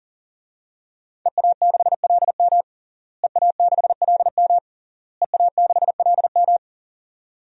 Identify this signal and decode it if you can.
Morse code, heard as a keyed tone, about 40 wpm, 700 Hz. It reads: EA6LM EA6LM EA6LM
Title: DXCC-100 Call Signs - 5 Characters - Repeated 3x - 40wpm